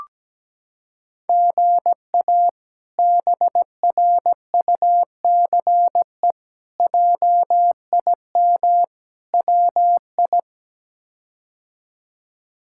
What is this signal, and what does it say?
Morse code, heard as a keyed tone, about 17 wpm, 700 Hz. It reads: GA BRUCE JIM WI
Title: K1USN SST QSO Practice - 17wpm